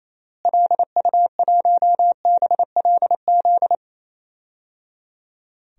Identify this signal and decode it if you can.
Morse code, heard as a keyed tone, about 28 wpm, 700 Hz. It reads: LU1BLZ
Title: DXCC-100 Call Signs - 6 Characters - Repeated 3x - 28wpm